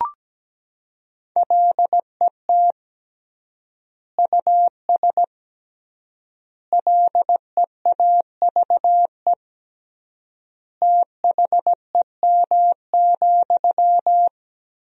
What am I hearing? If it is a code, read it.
Morse code, heard as a keyed tone, about 17 wpm, 700 Hz. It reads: LET US LEAVE THEM,
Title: 3x Word-Spacing - Sentences from Top 400 Words - 17wpm